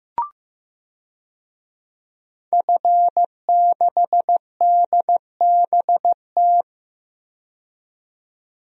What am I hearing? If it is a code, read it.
Morse code, heard as a keyed tone, about 15 wpm, 700 Hz. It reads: F6DBT
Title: DXCC-100 Call Signs - 5 Characters - 15wpm